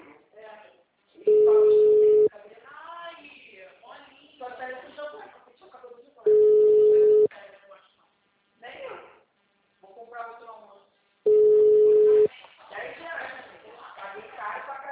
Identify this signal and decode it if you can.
Ringback tone (Europe, 425 Hz), with speech.